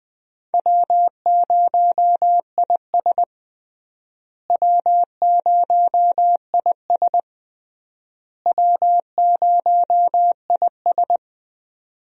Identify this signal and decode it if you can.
Morse code, heard as a keyed tone, about 20 wpm, 700 Hz. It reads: W0IS W0IS W0IS
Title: USA Call Signs - 4 Characters - Repeated 3x - 20wpm